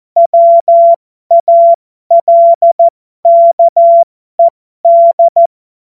Morse code, heard as a keyed tone, about 14 wpm, spaced as for 12 wpm, 680 Hz. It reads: WALKED
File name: Morsekurs / Witze